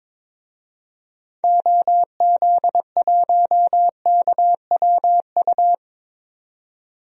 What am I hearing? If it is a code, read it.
Morse code, heard as a keyed tone, about 22 wpm, 700 Hz. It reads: OZ1KWU
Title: DXCC-100 Call Signs - 6 Characters - 22wpm